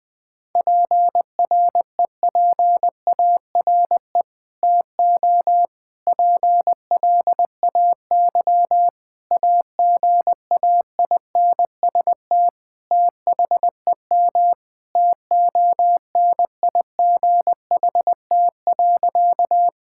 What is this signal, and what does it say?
Morse code, heard as a keyed tone, about 21 wpm, 700 Hz. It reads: PREPARE TO PLAY AGAINST THEM TONIGHT.
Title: Sentences from Top 1000 Words - Review Part 6 - 20wpm